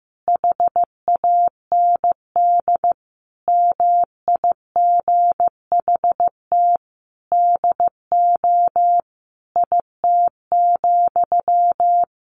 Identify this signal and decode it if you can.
Morse code, heard as a keyed tone, about 16 wpm, 700 Hz. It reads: HAND MIGHT DO IT,
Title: A Sound Of Thunder 15wpm